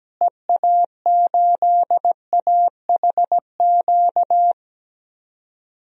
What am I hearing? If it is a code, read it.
Morse code, heard as a keyed tone, about 17 wpm, 700 Hz. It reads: EA8AHQ
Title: DXCC-100 Call Signs - 6 Characters - Repeated 3x - 17wpm